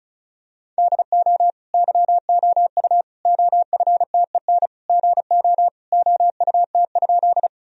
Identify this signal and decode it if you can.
Morse code, heard as a keyed tone, about 34 wpm, 700 Hz. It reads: DO YOU OFTEN GO OUT?
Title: Sentences from Top 1000 Words - Review Part 1 - 35wpm